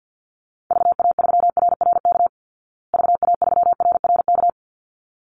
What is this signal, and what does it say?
Morse code, heard as a keyed tone, about 48 wpm, 700 Hz. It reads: VA3RRK VA3RRK
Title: Call Signs Repeated 3x - 50wpm